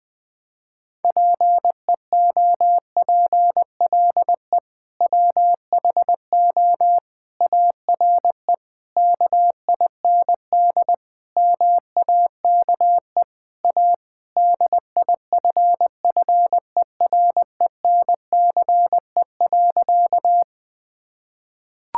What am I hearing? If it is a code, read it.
Morse code, heard as a keyed tone, about 20 wpm, 700 Hz. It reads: PEOPLE WHO ARE KIND MAKE A DIFFERENCE.
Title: Sentences with 3-Word Phrases - 20wpm